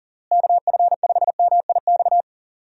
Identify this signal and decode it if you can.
Morse code, heard as a keyed tone, about 40 wpm, 700 Hz. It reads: KF5MIX
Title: USA Call Signs - 6 Characters - Repeated 3x - 40wpm